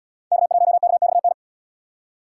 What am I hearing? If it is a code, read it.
Morse code, heard as a keyed tone, about 77 wpm, 700 Hz. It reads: K1KXN